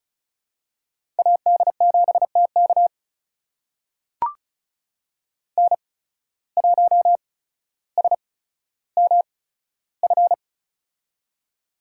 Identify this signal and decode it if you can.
Morse code, heard as a keyed tone, about 36 wpm, spaced as for 15 wpm, 700 Hz. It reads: AD7TK N1SMF
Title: USA Call Signs - 5 Characters - 8x Character Spacing - 35wpm